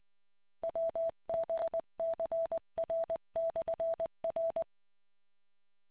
Morse code, heard as a keyed tone, about 20 wpm, 660 Hz. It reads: WBCR/R